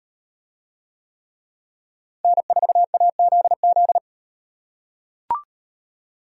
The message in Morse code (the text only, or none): N4AZZ